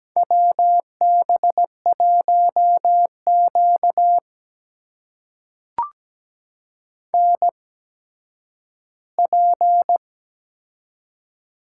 Morse code, heard as a keyed tone, about 17 wpm, spaced as for 10 wpm, 700 Hz. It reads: WB1Q NP
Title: USA Call Signs - 4 Characters - 8x Character Spacing - 17wpm